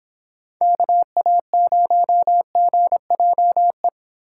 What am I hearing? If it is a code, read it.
Morse code, heard as a keyed tone, about 30 wpm, 700 Hz. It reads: KA0GJE